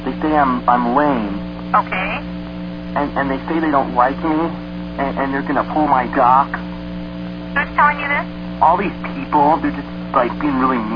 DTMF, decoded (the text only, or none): none